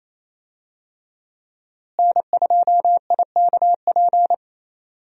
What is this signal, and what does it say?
Morse code, heard as a keyed tone, about 27 wpm, 700 Hz. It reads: N2IKP
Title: USA Call Signs - 5 Characters - Repeated 3x - 28wpm